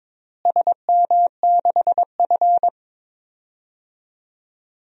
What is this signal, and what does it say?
Morse code, heard as a keyed tone, about 22 wpm, 700 Hz. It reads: SM6F
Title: DXCC-100 Call Signs - 4 Characters - Repeated 3x - 22wpm